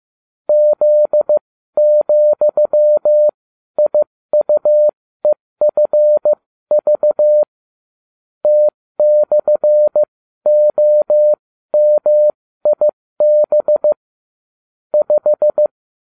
Morse code, heard as a keyed tone, about 15 wpm, spaced as for 13 wpm, 600 Hz. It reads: Z,IUEFV T/OMIB 5